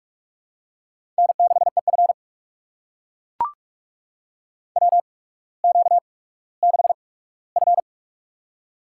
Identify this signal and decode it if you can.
Morse code, heard as a keyed tone, about 44 wpm, spaced as for 19 wpm, 700 Hz. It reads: N6EF WQ6F
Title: USA Call Signs - 4 Characters - 8x Character Spacing - 45wpm